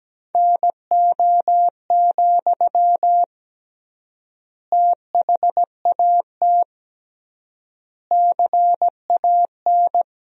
Morse code, heard as a keyed tone, about 17 wpm, 700 Hz. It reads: NO, THAT CAN
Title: 3x Word-Spacing - Sentences from Top 300 Words - 17wpm